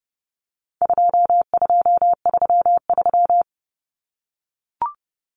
Morse code, heard as a keyed tone, about 30 wpm, 700 Hz. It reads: 2233
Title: Common Cut Numbers - 4-Digit 30wpm